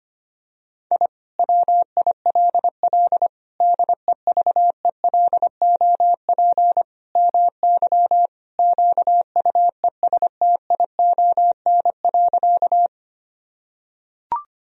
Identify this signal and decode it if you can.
Morse code, heard as a keyed tone, about 25 wpm, 700 Hz. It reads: I WILL DEVELOP MY QUESTION.
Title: Sentences from Top 1000 Words - Review Part 2 - 25wpm